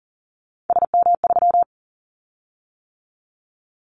Morse code, heard as a keyed tone, about 40 wpm, 700 Hz. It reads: SM3
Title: Callsign Prefixes for Top 25 DXCC - 40wpm